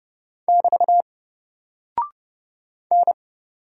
Morse code, heard as a keyed tone, about 31 wpm, 700 Hz. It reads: <BT> N
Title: Single Character (Letter:Number:Punctuation:Prosign) 30wpm